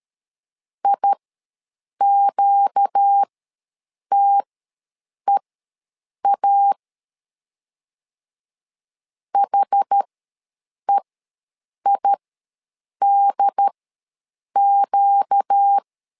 Morse code, heard as a keyed tone, about 13 wpm, spaced as for 7 wpm, 800 Hz. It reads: IQTEA HEIDQ
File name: LEZIONI / 4 Q ws 12wpm 1912